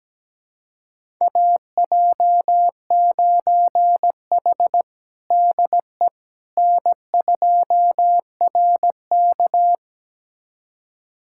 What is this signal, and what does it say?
Morse code, heard as a keyed tone, about 17 wpm, 700 Hz. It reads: AJ9H DE N2RK
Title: USA Callsign DE Callsign - 4 Characters - 17wpm